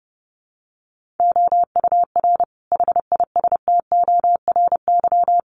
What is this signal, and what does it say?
Morse code, heard as a keyed tone, about 30 wpm, 700 Hz. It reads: OUR HISTORY
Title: Sentences from Top 1500 Words 30wpm